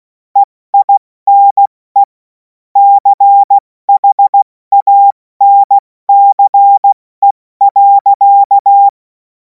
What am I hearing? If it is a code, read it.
Morse code, heard as a keyed tone, about 16 wpm, 800 Hz. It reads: EINE CHANCE.
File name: Morsekurs / Witze / Witz 03. März 2026 70 F80